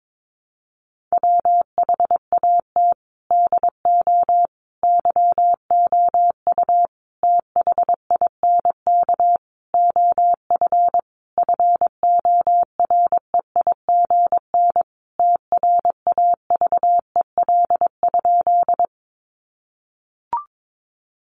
Morse code, heard as a keyed tone, about 22 wpm, 700 Hz. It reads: WHAT DO YOU THINK OF FOREIGN TRAVEL?
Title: Sentences from Top 1400 Words 22wpm